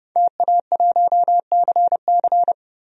Morse code, heard as a keyed tone, about 32 wpm, 700 Hz. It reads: TA1CC
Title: DXCC-100 Call Signs - 5 Characters - 30wpm